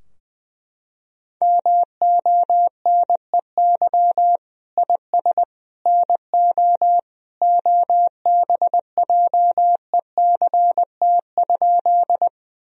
Morse code, heard as a keyed tone, about 20 wpm, 700 Hz. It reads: MONEY IS NO OBJECT?
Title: Sentences from Top 1000 Words - Review Part 3 - 20wpm